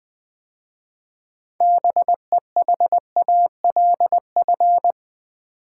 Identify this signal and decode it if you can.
Morse code, heard as a keyed tone, about 20 wpm, 700 Hz. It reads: BEHALF
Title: 6 Letter Words - First 3 Letters Given - 20wpm